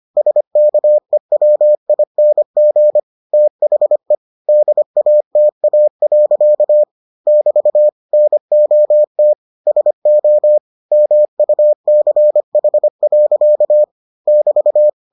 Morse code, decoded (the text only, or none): SKEWING THE DATA. <BT> NOT SO MUCH. <BT>